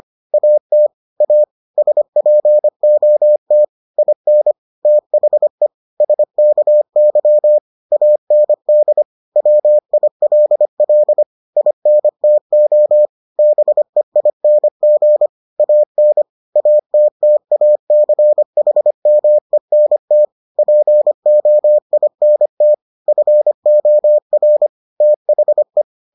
Morse code, heard as a keyed tone, about 25 wpm, 600 Hz. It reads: AT A SPOT IN THE SKY AND WILL INTO BEING AN ATTACHMENT POINT FOR THE